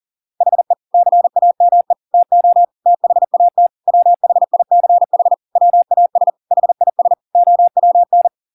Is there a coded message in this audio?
Yes: Morse code from 0.4 to 8.3 s, heard as a keyed tone, about 40 wpm, 700 Hz. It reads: HE CAME TO THAT WHICH WAS HIS OWN